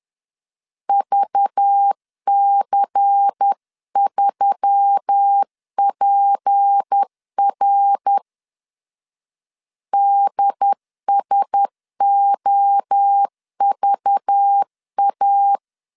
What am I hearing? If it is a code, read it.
Morse code, heard as a keyed tone, about 10 wpm, 800 Hz. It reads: VC3PR DSOVA